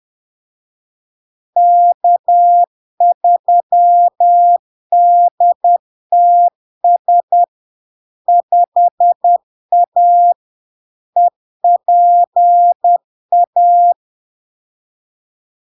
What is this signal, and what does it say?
Morse code, heard as a keyed tone, about 10 wpm, 700 Hz. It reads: K3DTS 5A EPA
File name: LARK-FD-2022-Log-in-CW-10wpm-700-Hz